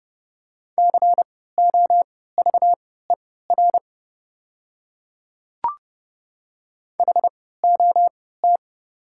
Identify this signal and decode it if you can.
Morse code, heard as a keyed tone, about 30 wpm, spaced as for 17 wpm, 700 Hz. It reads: COVER HOT
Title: 3x Character Spacing - Top 300 Words - 30wpm